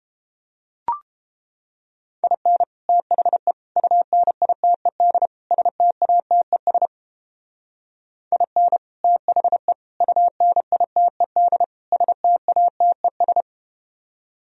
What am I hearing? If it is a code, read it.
Morse code, heard as a keyed tone, about 30 wpm, 700 Hz. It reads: IN THE UNITED STATES IN THE UNITED STATES